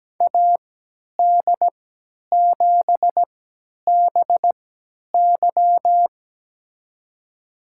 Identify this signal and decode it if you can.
Morse code, heard as a keyed tone, about 17 wpm, spaced as for 10 wpm, 700 Hz. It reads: AD7BY
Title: USA Call Signs - 5 Characters - 3x Character Spacing - 17wpm